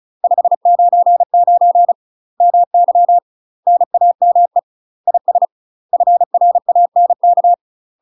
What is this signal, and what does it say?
Morse code, heard as a keyed tone, about 34 wpm, 700 Hz. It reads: 599 MY NAME IS FRANK
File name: UK MASTER QSOs TRAINING / 35WPM / UK MASTER QSOs 0025